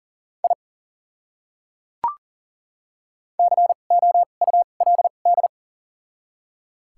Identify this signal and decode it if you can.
Morse code, heard as a keyed tone, about 40 wpm, spaced as for 29 wpm, 700 Hz. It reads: I COULD
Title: 2x Character Spacing - Top 100 Words - 40wpm